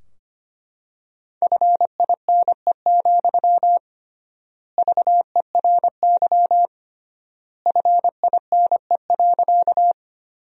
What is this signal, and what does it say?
Morse code, heard as a keyed tone, about 25 wpm, 700 Hz. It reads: FINE, VERY FINE.
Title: 3x Word-Spacing - Sentences from Top 500 Words - 25wpm